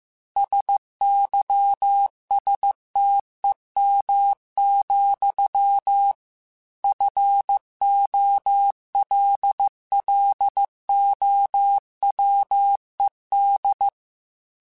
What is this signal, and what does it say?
Morse code, heard as a keyed tone, about 15 wpm, 800 Hz. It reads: SYSTEM, FOLLOWED
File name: Technical Subjects / Technical Subjects-mp3s / Sw Pwr Sup @13